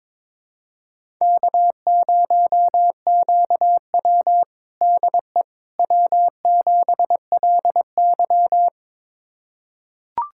Morse code, heard as a keyed tone, about 22 wpm, 700 Hz. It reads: K0QW DE W7LY
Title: USA Callsign DE Callsign - 4 Characters - 22wpm